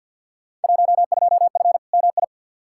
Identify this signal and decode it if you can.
Morse code, heard as a keyed tone, about 47 wpm, 700 Hz. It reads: 12F MI
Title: Field Day Exchanges Encore - 50wpm